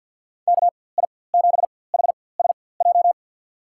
Morse code, heard as a keyed tone, about 46 wpm, spaced as for 23 wpm, 700 Hz. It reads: KI7HSJ
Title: USA Call Signs - 6 Characters - 4x Character Spacing - 50wpm